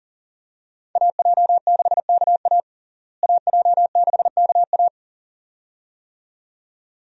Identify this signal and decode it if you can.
Morse code, heard as a keyed tone, about 40 wpm, 700 Hz. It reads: AJ6KA AJ6KA
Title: USA Call Signs - 5 Characters - Repeated 2x - 40wpm